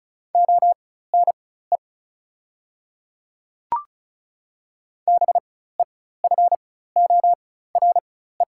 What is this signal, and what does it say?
Morse code, heard as a keyed tone, about 36 wpm, spaced as for 16 wpm, 690 Hz. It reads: ONE BEFORE
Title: 4x Character Spacing - Top 200 Words - 35wpm